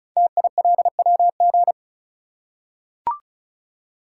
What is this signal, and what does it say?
Morse code, heard as a keyed tone, about 36 wpm, 700 Hz. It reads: TILWG
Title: Code Groups - 5 Letters - 35wpm